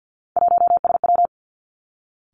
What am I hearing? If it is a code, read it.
Morse code, heard as a keyed tone, about 50 wpm, 700 Hz. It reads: 1SW
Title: Three Challenging Letter-Number Combinations 50wpm